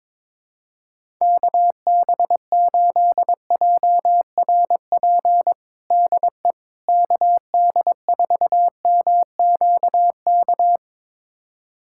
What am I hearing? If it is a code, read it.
Morse code, heard as a keyed tone, about 22 wpm, 700 Hz. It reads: KB8JRP DE KD4MQK
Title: USA Callsign DE Callsign - 6 Characters - 22wpm